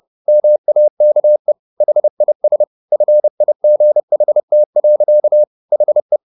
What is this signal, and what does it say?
Morse code, heard as a keyed tone, about 30 wpm, 600 Hz. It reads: MAKE HIS FIGHT. HE
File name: the old man and the sea / the old man and the sea 1 30wpm